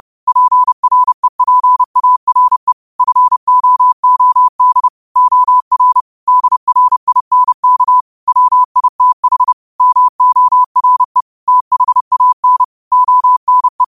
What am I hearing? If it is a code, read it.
Morse code, heard as a keyed tone, about 30 wpm, 1000 Hz. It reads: PREPARE FOOD OR DRINK WITH MORE THAN ONE